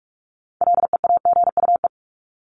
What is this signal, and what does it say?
Morse code, heard as a keyed tone, about 44 wpm, 700 Hz. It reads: LEAGUE
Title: Top 1751-2000 Words 45wpm